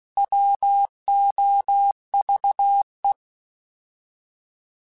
Morse code, heard as a keyed tone, about 16 wpm, 800 Hz. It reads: WOVE
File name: Alphabetical / Alphabetical mp3s / Ws @15